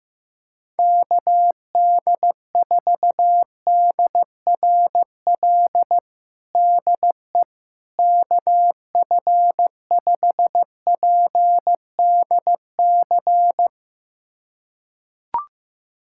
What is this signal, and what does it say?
Morse code, heard as a keyed tone, about 15 wpm, 700 Hz. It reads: KD4DRL DE KF5PDC
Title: USA Callsign DE Callsign - 6 Characters - 15wpm